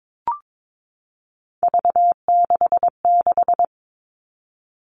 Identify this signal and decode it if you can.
Morse code, heard as a keyed tone, about 22 wpm, 700 Hz. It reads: V66
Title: All Cut Numbers - 3-Digit 22wpm